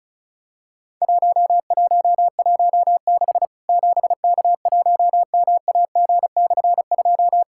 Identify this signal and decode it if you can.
Morse code, heard as a keyed tone, about 36 wpm, 700 Hz. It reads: 1116 7K1MAG/2
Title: Worked All Europe DX Contest - QTC Exchange - 35wpm